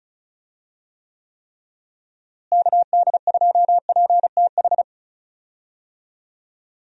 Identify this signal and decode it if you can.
Morse code, heard as a keyed tone, about 36 wpm, 700 Hz. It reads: KD2PTH
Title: USA Call Signs - 6 Characters - Repeated 3x - 35wpm